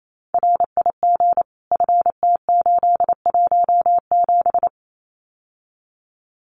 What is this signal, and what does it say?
Morse code, heard as a keyed tone, about 28 wpm, 700 Hz. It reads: RIG FT817
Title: Common QSO Phrases Encore 28wpm